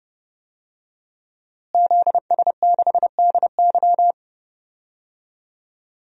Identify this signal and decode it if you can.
Morse code, heard as a keyed tone, about 30 wpm, 700 Hz. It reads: ZS6DY
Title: DXCC-100 Call Signs - 5 Characters - Repeated 2x - 30wpm